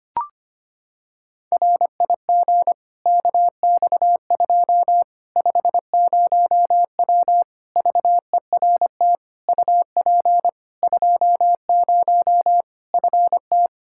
Morse code, heard as a keyed tone, about 25 wpm, 700 Hz. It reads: RIG KX2 50W VERT UP 20 FT
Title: Practice QSO - Exchange 2 - Info - 25wpm